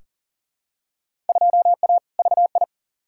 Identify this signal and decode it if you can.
Morse code, heard as a keyed tone, about 37 wpm, 700 Hz. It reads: 2A VI
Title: Field Day Exchanges Encore - 40wpm